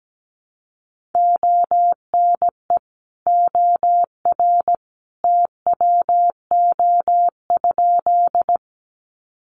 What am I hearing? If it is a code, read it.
Morse code, heard as a keyed tone, about 17 wpm, 700 Hz. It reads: ONE OR TWO?
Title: Sentences from Top 100 Words 17wpm